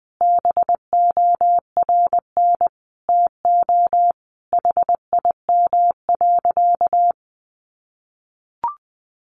Morse code, heard as a keyed tone, about 20 wpm, 700 Hz. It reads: BORN TO HIM.
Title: Sentences from Top 1500 Words 20wpm